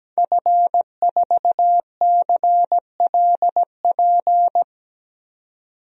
Morse code, heard as a keyed tone, about 17 wpm, 700 Hz. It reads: F4CLP
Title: DXCC-100 Call Signs - 5 Characters - 17wpm